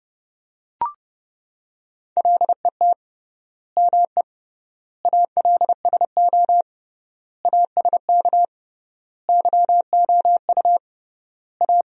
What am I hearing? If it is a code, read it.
Morse code, heard as a keyed tone, about 30 wpm, 700 Hz. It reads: LET ME ALSO ASK YOU A
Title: 3x Word-Spacing - Sentences from Top 400 Words - 30wpm